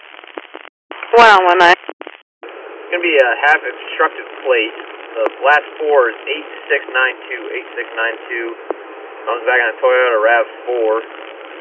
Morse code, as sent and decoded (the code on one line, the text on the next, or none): none